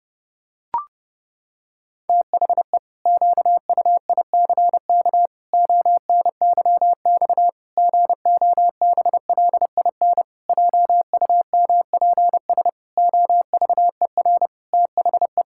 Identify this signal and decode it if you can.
Morse code, heard as a keyed tone, about 30 wpm, 700 Hz. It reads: THE QUICK ONYX GOBLIN JUMPS OVER THE
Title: Pangram Challenge v2 - 30wpm